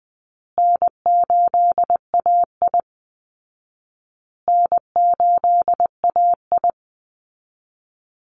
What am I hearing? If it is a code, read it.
Morse code, heard as a keyed tone, about 20 wpm, 700 Hz. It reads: N8AI N8AI
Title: Easy Call Signs Repeated 3x - 20wpm